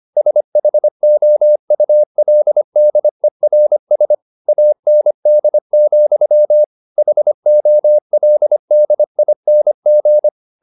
Morse code, heard as a keyed tone, about 25 wpm, 600 Hz. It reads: SHOULDERS AND, HOLDING
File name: the old man and the sea part 1 25wpm